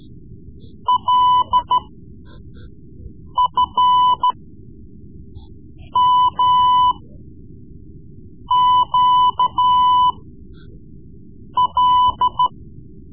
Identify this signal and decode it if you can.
Morse code, heard as a keyed tone, about 13 wpm, spaced as for 5 wpm, 1000 Hz. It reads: LFMQL
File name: Morse Lessons / Morse Lessons / Tape-3B